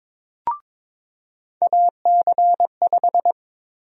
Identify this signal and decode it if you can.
Morse code, heard as a keyed tone, about 22 wpm, 700 Hz. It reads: AC5
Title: USA Call Sign Prefixes - 22wpm